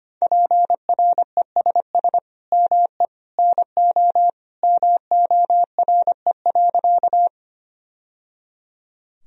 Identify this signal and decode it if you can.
Morse code, heard as a keyed tone, about 25 wpm, 700 Hz. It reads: PRESS ME NO MORE.
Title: Sentences from Top 1000 Words - Review Part 1 - 25wpm